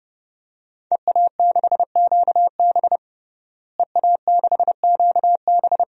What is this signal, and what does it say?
Morse code, heard as a keyed tone, about 30 wpm, 700 Hz. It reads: EA6QB EA6QB
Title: DXCC-100 Call Signs - 5 Characters - Repeated 2x - 30wpm